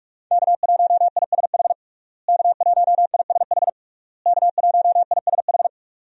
Morse code, heard as a keyed tone, about 46 wpm, 700 Hz. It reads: K1ISH K1ISH K1ISH
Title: USA Call Signs - 5 Characters - Repeated 3x - 45wpm